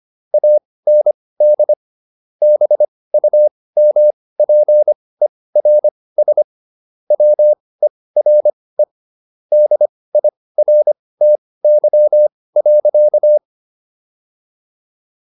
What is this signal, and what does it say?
Morse code, heard as a keyed tone, about 25 wpm, spaced as for 18 wpm, 600 Hz. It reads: AND BUMPERS WERE DIRTY.